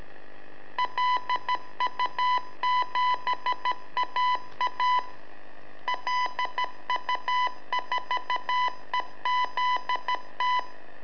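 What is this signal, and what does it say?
Morse code, heard as a keyed tone, about 13 wpm, 1000 Hz. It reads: LU7AA LU4EZT
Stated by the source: Telemetry in Morse from LU7AA and LU4EZ, AMSAT and Turma Mar del Plata, sent from a balloon.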